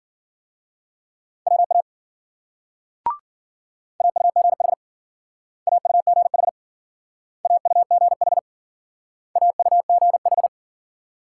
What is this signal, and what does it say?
Morse code, heard as a keyed tone, about 52 wpm, 700 Hz. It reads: WA AUGH AUGH AUGH AUGH